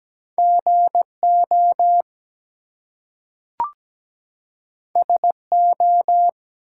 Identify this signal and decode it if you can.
Morse code, heard as a keyed tone, about 17 wpm, 700 Hz. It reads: GO SO
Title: Two Letter Words - 17wpm